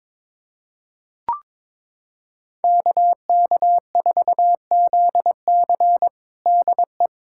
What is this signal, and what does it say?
Morse code, heard as a keyed tone, about 22 wpm, 700 Hz. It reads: KK4ZC DE
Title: USA Callsign DE Callsign - 5 Characters - 22wpm